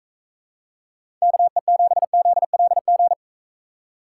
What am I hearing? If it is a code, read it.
Morse code, heard as a keyed tone, about 42 wpm, 700 Hz. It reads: KE7ZLG